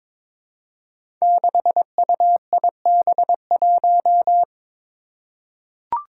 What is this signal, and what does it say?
Morse code, heard as a keyed tone, about 22 wpm, 700 Hz. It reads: 6UIB1
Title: Five Challenging Letter-Number Combinations 22wpm